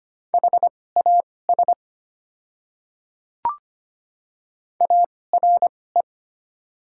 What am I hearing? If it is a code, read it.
Morse code, heard as a keyed tone, about 25 wpm, spaced as for 18 wpm, 700 Hz. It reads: HAS ARE
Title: 2x Character Spacing - Top 100 Words - 25wpm